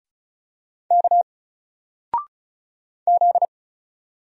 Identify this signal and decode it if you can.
Morse code, heard as a keyed tone, about 33 wpm, 700 Hz. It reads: K Z
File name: Single-Letter-Number-35wpm